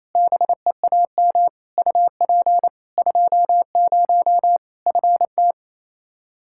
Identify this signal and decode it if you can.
Morse code, heard as a keyed tone, about 28 wpm, 700 Hz. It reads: BEAM UP 20 FT
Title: Practice QSO - Exchange 2 - Info - 28wpm